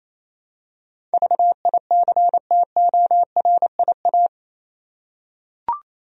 Morse code, heard as a keyed tone, about 28 wpm, 700 Hz. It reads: VICTORIA
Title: USA Common Names - 28wpm